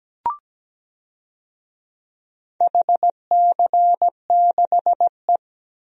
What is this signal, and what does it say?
Morse code, heard as a keyed tone, about 17 wpm, 700 Hz. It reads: HC6E
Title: DXCC-100 Call Signs - 4 Characters - 17wpm